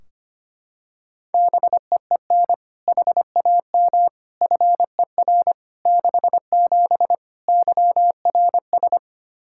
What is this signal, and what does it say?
Morse code, heard as a keyed tone, about 25 wpm, 700 Hz. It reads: BEEN HAM FER 67 YRS
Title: Practice QSO - Exchange 3 - More Info - 25wpm